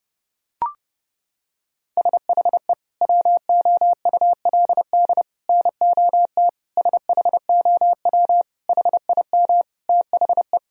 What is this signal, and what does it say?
Morse code, heard as a keyed tone, about 32 wpm, 700 Hz. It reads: SHE WOULD NOT SHOW HIM THE
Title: Sentences from Top 1000 Words - Review Part 3 - 30wpm